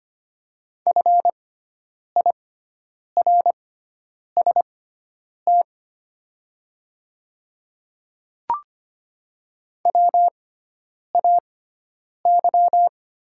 Morse code, heard as a keyed tone, about 25 wpm, spaced as for 9 wpm, 690 Hz. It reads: FIRST WAY
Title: 6x Character Spacing - Top 200 Words - 25wpm